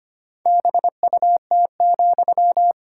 Morse code, heard as a keyed tone, about 25 wpm, 700 Hz. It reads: BUT,
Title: Sentences from Top 1000 Words - Review Part 6 - 25wpm